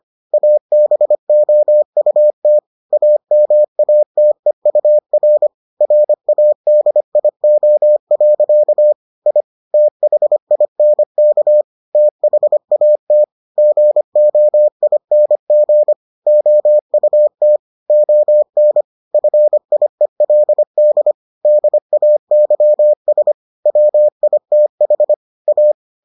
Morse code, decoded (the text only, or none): ABOUT AMATEUR RADIO. I THINK THAT GOING OUT ON FIELD DAYS WITH A